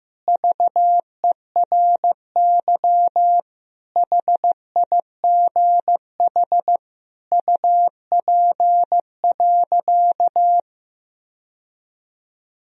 Morse code, heard as a keyed tone, about 15 wpm, 700 Hz. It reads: VERY HIGH UP.
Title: Sentences from Top 1000 Words - Review Part 1 - 15wpm